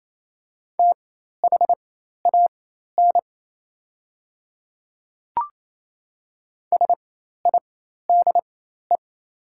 Morse code, heard as a keyed tone, about 28 wpm, spaced as for 13 wpm, 700 Hz. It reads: THAN SIDE
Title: 4x Character Spacing - Top 200 Words - 28wpm